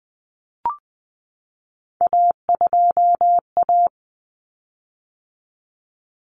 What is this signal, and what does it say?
Morse code, heard as a keyed tone, about 20 wpm, 700 Hz. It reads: A2A
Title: Common Cut Numbers - 3-Digit 20wpm